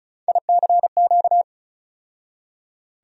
Morse code, heard as a keyed tone, about 36 wpm, 700 Hz. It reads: ICQ
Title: USA Call Sign Postfixes - 3 Letters 35wpm